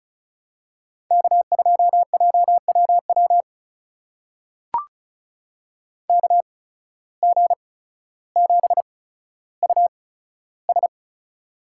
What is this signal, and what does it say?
Morse code, heard as a keyed tone, about 36 wpm, spaced as for 15 wpm, 700 Hz. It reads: K2JWW KG7US
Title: USA Call Signs - 8x Character Spacing - 35wpm